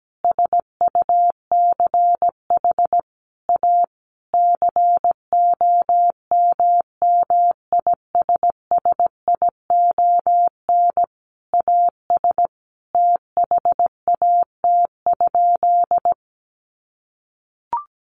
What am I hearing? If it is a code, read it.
Morse code, heard as a keyed tone, about 17 wpm, 700 Hz. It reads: SUCH A COMMISSION AS THAT?
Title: Sentences from Top 1400 Words 17wpm